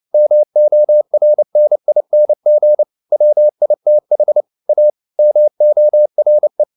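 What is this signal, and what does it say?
Morse code, heard as a keyed tone, about 29 wpm, 600 Hz. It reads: MORNING WITH A MORE
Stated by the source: Morse code at 29 wpm, sending Ken Follett's novel Never in English, from CW med Gnister.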